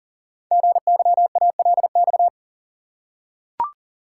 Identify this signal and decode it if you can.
Morse code, heard as a keyed tone, about 42 wpm, 690 Hz. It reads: GYALX
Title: Code Groups - 5 Letters - 40wpm